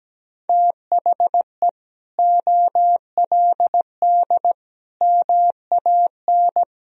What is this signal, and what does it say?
Morse code, heard as a keyed tone, about 17 wpm, 700 Hz. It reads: THE OLD MAN
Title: Sentences from Top 500 Words and Abbreviations 17wpm